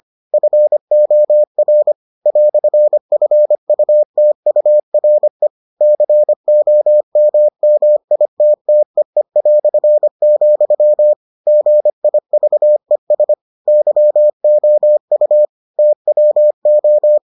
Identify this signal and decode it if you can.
Morse code, heard as a keyed tone, about 25 wpm, 600 Hz. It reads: FOR "FUTURE COMMITTEE", GIVES YOU TWO